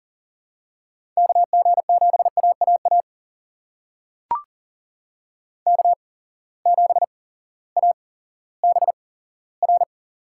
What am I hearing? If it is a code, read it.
Morse code, heard as a keyed tone, about 40 wpm, spaced as for 18 wpm, 700 Hz. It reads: KG7AAA K7ABR
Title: USA Call Signs - 8x Character Spacing - 40wpm